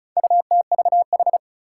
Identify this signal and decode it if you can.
Morse code, heard as a keyed tone, about 36 wpm, 700 Hz. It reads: UTVH
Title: Four Challenging Letter-Number Combinations 35wpm